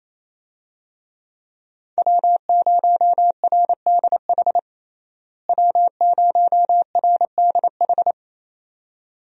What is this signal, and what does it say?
Morse code, heard as a keyed tone, about 27 wpm, 700 Hz. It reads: W0RDH W0RDH
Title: USA Call Signs - 5 Characters - Repeated 2x - 28wpm